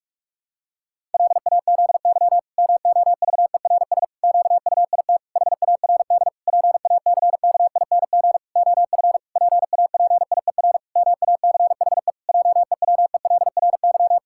Morse code, heard as a keyed tone, about 46 wpm, 700 Hz. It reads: LAZY MOVERS QUIT HARD PACKING OF PAPIER MACHE JEWELRY
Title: Pangram Challenge v2 - 45wpm